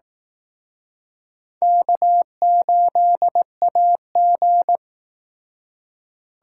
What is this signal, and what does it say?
Morse code, heard as a keyed tone, about 19 wpm, 700 Hz. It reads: K8AG
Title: Speed Racing - USA 1x2 Call Signs 15wpm